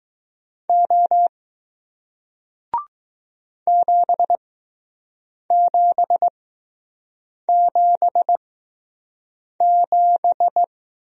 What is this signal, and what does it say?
Morse code, heard as a keyed tone, about 20 wpm, spaced as for 6 wpm, 700 Hz. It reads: O 7777